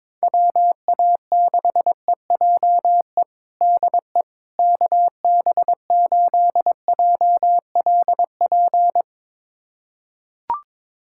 Morse code, heard as a keyed tone, about 22 wpm, 700 Hz. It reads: WA6EJE DE KB8JLP
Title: USA Callsign DE Callsign - 6 Characters - 22wpm